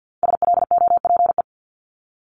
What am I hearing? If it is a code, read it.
Morse code, heard as a keyed tone, about 50 wpm, 690 Hz. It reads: SLOPE
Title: Multi-letter Phonemes 50wpm